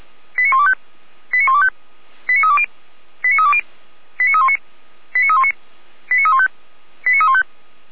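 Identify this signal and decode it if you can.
Five-tone selective call (ZVEI), heard as quick runs of five tones, tones of about 70 ms. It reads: 89125 89125 89320 89320 89319 89319 89315 89315
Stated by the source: ZVEI five-tone selective call received by SDR.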